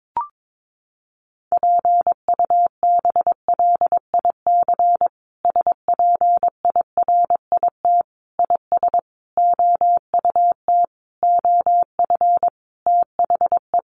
Morse code, heard as a keyed tone, about 22 wpm, 700 Hz. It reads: PUBLIC SPIRIT IS OUT OF THE
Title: Sentences from Top 1400 Words 22wpm